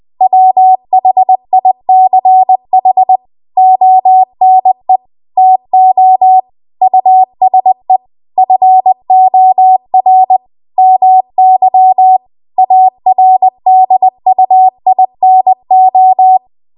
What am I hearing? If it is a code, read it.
Morse code, heard as a keyed tone, about 20 wpm, 750 Hz. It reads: WHICH ONE TO USE FOR MY ARDUINO